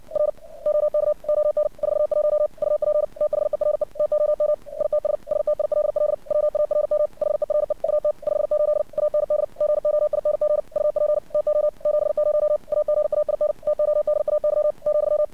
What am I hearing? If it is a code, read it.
Morse code, heard as a keyed tone, about 56 wpm, 600 Hz. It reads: R OK ON 41 UP THERE TOM ITS STILL WARM HERE AT 58 AND GOING UP TO 70 AGAIN TODAY ,
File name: archived / CRSnet